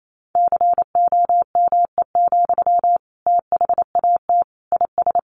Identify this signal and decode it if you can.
Morse code, heard as a keyed tone, about 28 wpm, 700 Hz. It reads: COME, THAT IS
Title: Sentences from Top 700 Words 28wpm